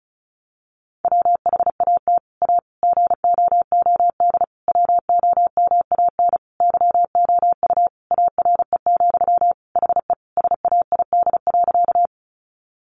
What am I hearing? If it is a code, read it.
Morse code, heard as a keyed tone, about 34 wpm, 700 Hz. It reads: WHAT A GOOD WOMAN YOU ARE, HE SAID.
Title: Sentences from Top 1100 Words 35wpm